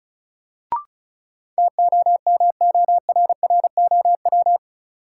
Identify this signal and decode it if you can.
Morse code, heard as a keyed tone, about 36 wpm, 700 Hz. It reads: TOMORROW
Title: Sentences from Top 1100 Words - New Word Drill Alt - 35wpm